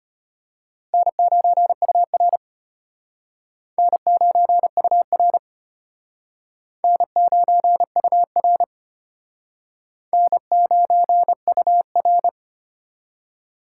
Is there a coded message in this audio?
Yes: Morse code from 0.9 to 12.3 s, heard as a keyed tone, about 32 wpm, 700 Hz. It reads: N9UR N9UR N9UR N9UR